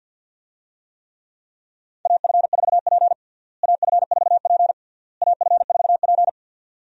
Morse code, heard as a keyed tone, about 46 wpm, 700 Hz. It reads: AF4P AF4P AF4P
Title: USA Call Signs - 4 Characters - Repeated 3x - 50wpm